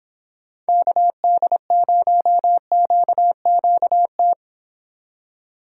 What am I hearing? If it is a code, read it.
Morse code, heard as a keyed tone, about 24 wpm, 700 Hz. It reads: KD0QQT